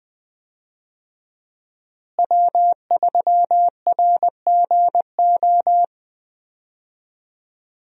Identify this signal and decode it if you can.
Morse code, heard as a keyed tone, about 20 wpm, 700 Hz. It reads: W3RGO
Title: USA Call Signs - 5 Characters - Repeated 3x - 20wpm